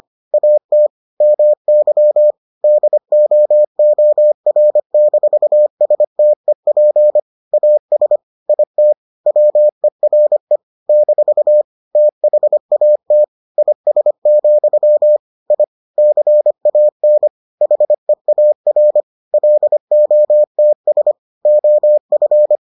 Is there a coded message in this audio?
Yes: Morse code from 0.3 to 22.6 s, heard as a keyed tone, about 25 wpm, 600 Hz. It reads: AT MY DOOR-STEP AS IT WERE - THAT IS, I CAN HEAR LOTS OF